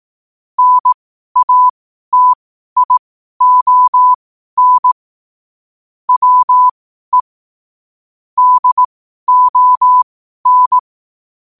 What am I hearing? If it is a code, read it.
Morse code, heard as a keyed tone, about 18 wpm, spaced as for 12 wpm, 1000 Hz. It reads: NATION WE DON
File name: qotdfiles / 12wpm